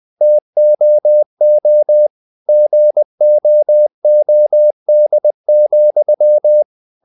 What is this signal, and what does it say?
Morse code, heard as a keyed tone, about 20 wpm, 600 Hz. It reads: TOO GOOD,
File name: CW / Practice / If20